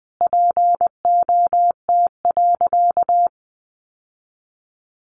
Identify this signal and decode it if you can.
Morse code, heard as a keyed tone, about 19 wpm, 700 Hz. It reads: POT.
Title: Sentences from Top 1000 Words 20wpm